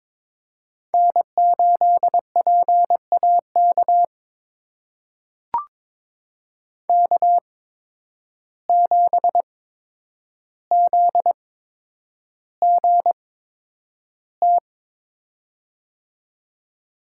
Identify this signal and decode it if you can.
Morse code, heard as a keyed tone, about 22 wpm, spaced as for 9 wpm, 700 Hz. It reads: N8PAK K7ZGT
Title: USA Call Signs - 5 Characters - 8x Character Spacing - 22wpm